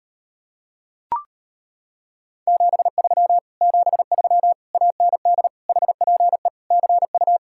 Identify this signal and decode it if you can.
Morse code, heard as a keyed tone, about 33 wpm, 700 Hz. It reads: 73 73 AND HPE CU